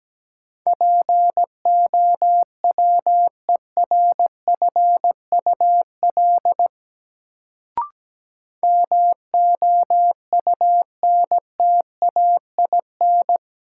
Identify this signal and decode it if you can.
Morse code, heard as a keyed tone, about 17 wpm, 700 Hz. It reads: POWERFUL MOUNTAIN
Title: Sentences from Top 1300 Words - New Word Drill Alt - 17wpm